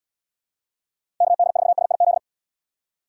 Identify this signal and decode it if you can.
Morse code, heard as a keyed tone, about 72 wpm, 700 Hz. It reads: KD4SIZ